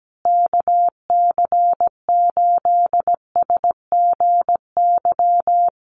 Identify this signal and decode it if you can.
Morse code, heard as a keyed tone, about 17 wpm, 700 Hz. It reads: KC8SGY
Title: USA Call Signs Repeated 2x 17wpm